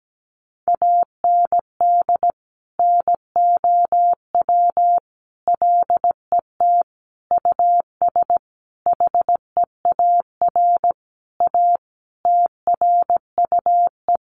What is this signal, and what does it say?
Morse code, heard as a keyed tone, about 17 wpm, 700 Hz. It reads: AND NOW LET US HEAR A TRUE
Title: Sentences from Top 900 Words 17wpm